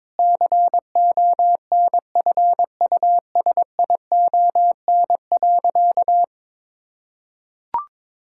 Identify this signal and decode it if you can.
Morse code, heard as a keyed tone, about 22 wpm, 700 Hz. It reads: CONFUSION.
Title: Sentences from Top 1600 Words 22wpm